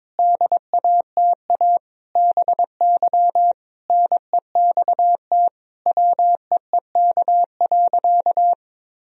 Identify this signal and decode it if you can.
Morse code, heard as a keyed tone, about 22 wpm, 700 Hz. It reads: DATA BY NEXT WEEK.
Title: Sentences from Top 1000 Words - Review Part 3 - 22wpm